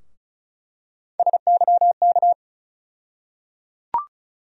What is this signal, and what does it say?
Morse code, heard as a keyed tone, about 36 wpm, 700 Hz. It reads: SYK
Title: USA Call Sign Postfixes - 3 Letters 35wpm